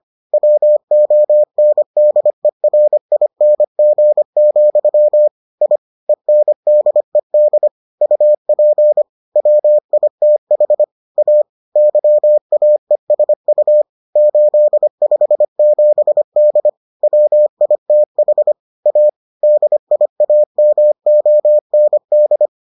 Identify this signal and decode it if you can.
Morse code, heard as a keyed tone, about 25 wpm, 600 Hz. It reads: WONDERING, I ENDED UP WITH A YAESU 857D WITH A DIAMOND